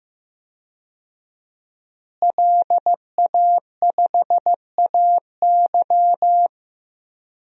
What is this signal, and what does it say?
Morse code, heard as a keyed tone, about 15 wpm, 700 Hz. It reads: LA5AY
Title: DXCC-100 Call Signs - 5 Characters - Repeated 2x - 15wpm